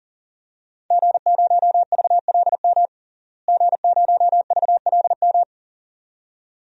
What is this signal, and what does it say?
Morse code, heard as a keyed tone, about 40 wpm, 700 Hz. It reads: G0VLM G0VLM
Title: DXCC-100 Call Signs - 5 Characters - Repeated 2x - 40wpm